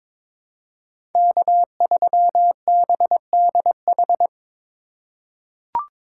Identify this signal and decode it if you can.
Morse code, heard as a keyed tone, about 22 wpm, 700 Hz. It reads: K3BDH
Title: USA Call Signs - 3x Character Spacing - 22wpm